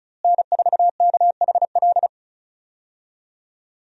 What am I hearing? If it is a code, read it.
Morse code, heard as a keyed tone, about 34 wpm, 700 Hz. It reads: N4KHL
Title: USA Call Signs - 5 Characters - Repeated 2x - 35wpm